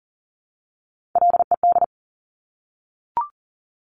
Morse code, heard as a keyed tone, about 38 wpm, 700 Hz. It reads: LED
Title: Top Three Letter Combinations in Words 40wpm